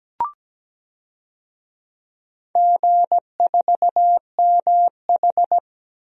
Morse code, heard as a keyed tone, about 17 wpm, 700 Hz. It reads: G4MH
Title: DXCC-100 Call Signs - 4 Characters - Repeated 3x - 17wpm